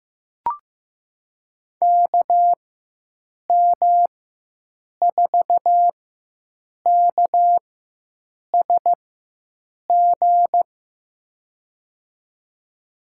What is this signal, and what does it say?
Morse code, heard as a keyed tone, about 15 wpm, spaced as for 7 wpm, 700 Hz. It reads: KM4KSG
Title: USA Call Signs - 6 Characters - 4x Character Spacing - 15wpm